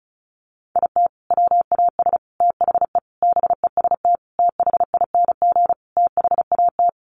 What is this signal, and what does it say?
Morse code, heard as a keyed tone, about 34 wpm, 700 Hz. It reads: IT WAS THE BEST THING THAT
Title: Sentences from Top 400 Words 35wpm